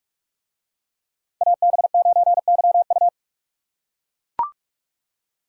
Morse code, heard as a keyed tone, about 40 wpm, 700 Hz. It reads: AB9YU